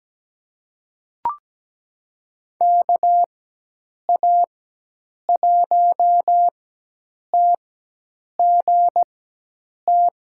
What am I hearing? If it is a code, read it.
Morse code, heard as a keyed tone, about 17 wpm, spaced as for 8 wpm, 700 Hz. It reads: KA1TGT
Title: USA Call Signs - 6 Characters - 4x Character Spacing - 17wpm